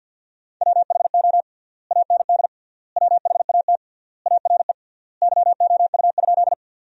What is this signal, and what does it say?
Morse code, heard as a keyed tone, about 52 wpm, 700 Hz. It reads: WHO AND WHAT ARE YOU?
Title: 3x Word-Spacing - Sentences from Top 100 Words - 50wpm